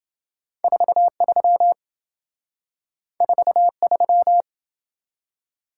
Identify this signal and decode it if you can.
Morse code, heard as a keyed tone, about 27 wpm, 700 Hz. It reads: 43 43